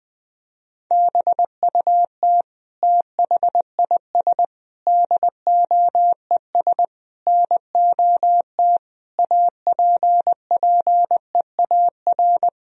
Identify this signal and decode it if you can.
Morse code, heard as a keyed tone, about 20 wpm, 700 Hz. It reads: BUT THIS DOES NOT APPEAR
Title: Sentences from Top 1000 Words - Review Part 2 - 20wpm